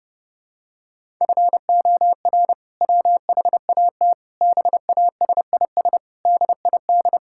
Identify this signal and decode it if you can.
Morse code, heard as a keyed tone, about 32 wpm, 700 Hz. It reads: FOR WHAT BASIS DID
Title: Sentences from Top 1000 Words - Review Part 5 - 30wpm